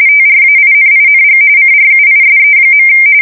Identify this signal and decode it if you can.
RTTY, about 45.45 baud. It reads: TRAINING CRUISE.
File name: fsk170 / USS-New-Jersey